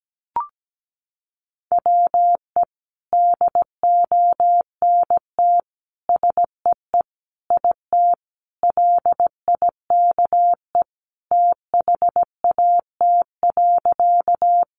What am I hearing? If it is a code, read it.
Morse code, heard as a keyed tone, about 17 wpm, 700 Hz. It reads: WE DONT SEE IT LIKE THAT.
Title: Sentences from Top 300 Words 17wpm